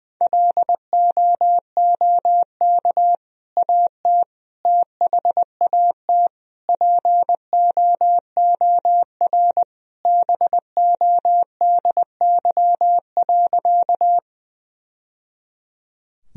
Morse code, heard as a keyed tone, about 20 wpm, 700 Hz. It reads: LOOK AT THAT POOR BODY.
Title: Sentences from Top 1000 Words - Review Part 2 - 20wpm